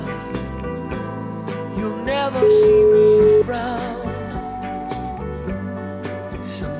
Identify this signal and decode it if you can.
Ringback tone (Europe, 425 Hz).